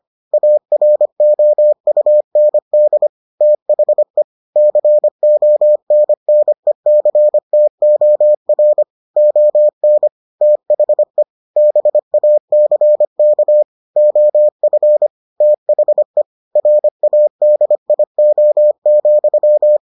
Morse code, heard as a keyed tone, about 25 wpm, 600 Hz. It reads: AROUND THE CONNECTOR ON THE BACK OF THE RADIO,